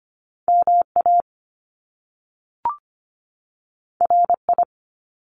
Morse code, heard as a keyed tone, about 25 wpm, 700 Hz. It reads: MA RI
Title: US States 25wpm